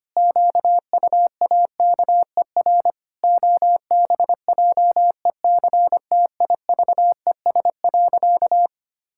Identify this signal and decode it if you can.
Morse code, heard as a keyed tone, about 25 wpm, 700 Hz. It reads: QUAKER OBJECTIVES.
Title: Pangram Challenge v2 - 25wpm